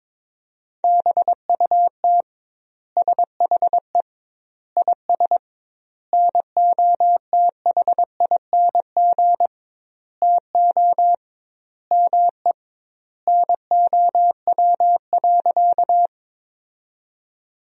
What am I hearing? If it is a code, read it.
Morse code, heard as a keyed tone, about 21 wpm, 700 Hz. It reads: BUT SHE IS NOTHING TO ME NOW.
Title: 2x Word-Spacing - Sentences from Top 500 Words - 22wpm